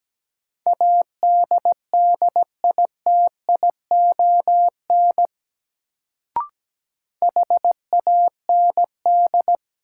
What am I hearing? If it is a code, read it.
Morse code, heard as a keyed tone, about 17 wpm, 700 Hz. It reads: ADDITION HAND
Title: Sets of 3 Words - New Word Drill - 17wpm